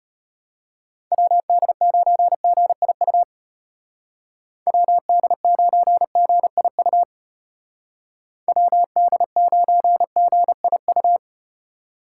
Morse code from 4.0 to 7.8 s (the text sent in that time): WD9GIU